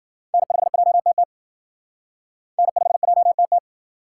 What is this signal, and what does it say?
Morse code, heard as a keyed tone, about 49 wpm, 700 Hz. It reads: N5JTT N5JTT
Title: Speed Racing - USA 1x3 Call Signs 40wpm